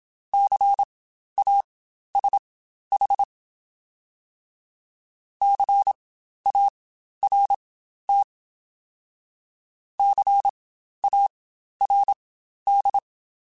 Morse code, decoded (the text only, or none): CASH CART CARD